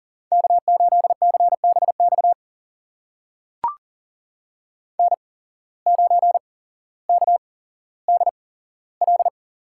Morse code, heard as a keyed tone, about 38 wpm, spaced as for 17 wpm, 700 Hz. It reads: K8CBX N9KDL
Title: USA Call Signs - 8x Character Spacing - 40wpm